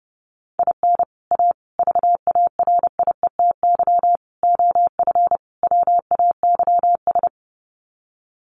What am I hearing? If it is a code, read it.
Morse code, heard as a keyed tone, about 30 wpm, 700 Hz. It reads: IN A VARIETY OF WAYS
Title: Sets of 5 Words Encore 30wpm